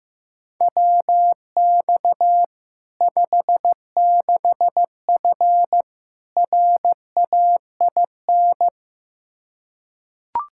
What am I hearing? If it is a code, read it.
Morse code, heard as a keyed tone, about 16 wpm, 700 Hz. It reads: WX 56F RAIN
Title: Practice QSO - Exchange 2 - Info - 15wpm